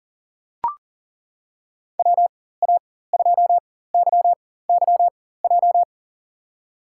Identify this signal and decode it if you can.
Morse code, heard as a keyed tone, about 40 wpm, spaced as for 19 wpm, 700 Hz. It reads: WA2YYJ
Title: USA Call Signs - 6 Characters - 4x Character Spacing - 40wpm